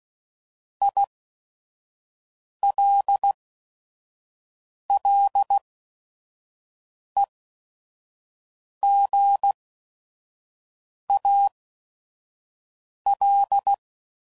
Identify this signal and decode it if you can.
Morse code, heard as a keyed tone, about 16 wpm, spaced as for 5 wpm, 800 Hz. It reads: ILLEGAL
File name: Prefixes / By First Letter / Is